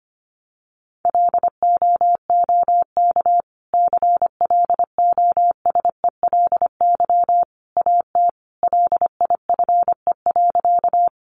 Morse code, heard as a keyed tone, about 25 wpm, 700 Hz. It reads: LOOK CLOSELY AT LIFE.
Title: Sentences from Top 1400 Words 25wpm